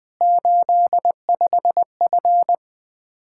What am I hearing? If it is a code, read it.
Morse code, heard as a keyed tone, about 20 wpm, 700 Hz. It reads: 85F
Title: Practice QSO - Exchange 2 - Info - 20wpm